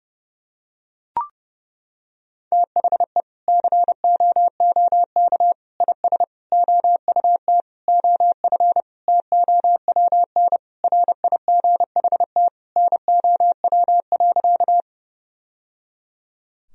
Morse code, heard as a keyed tone, about 30 wpm, 700 Hz. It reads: THE COOK IS OUT OF TOWN RIGHT NOW.
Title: Sentences from Top 1000 Words - Review Part 4 - 30wpm